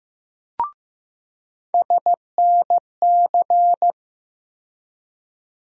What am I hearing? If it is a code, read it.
Morse code, heard as a keyed tone, about 15 wpm, 700 Hz. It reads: SNC
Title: USA Call Sign Postfixes - 3 Letters 15wpm